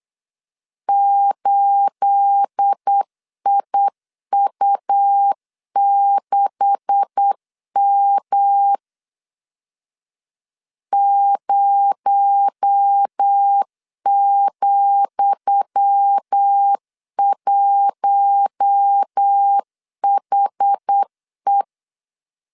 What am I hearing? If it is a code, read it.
Morse code, heard as a keyed tone, about 8 wpm, 800 Hz. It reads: 8IU6M 0,1HE
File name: LEZIONI / 10 SIGNS / 10D COMA 8wpm 3749